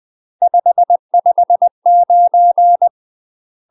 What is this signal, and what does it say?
Morse code, heard as a keyed tone, about 20 wpm, 700 Hz. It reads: 559